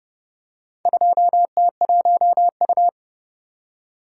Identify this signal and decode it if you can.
Morse code, heard as a keyed tone, about 28 wpm, 700 Hz. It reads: 2T1U